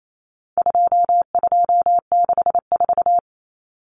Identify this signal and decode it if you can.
Morse code, heard as a keyed tone, about 28 wpm, 700 Hz. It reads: 2264
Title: Numbers - 4-Digit 28wpm